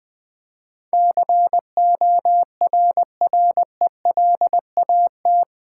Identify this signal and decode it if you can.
Morse code, heard as a keyed tone, about 20 wpm, 700 Hz. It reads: CORRELAT
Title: Build A Word - 4 Letters At A Time - 11-Letter Words - 20wpm